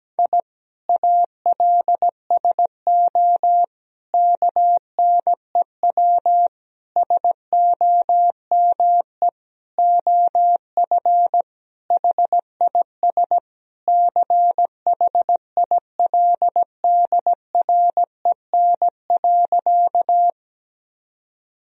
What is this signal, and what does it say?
Morse code, heard as a keyed tone, about 17 wpm, 700 Hz. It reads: I ALSO KNEW SOME OF HIS CHILDREN.
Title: Sentences from Top 1000 Words - Review Part 1 - 17wpm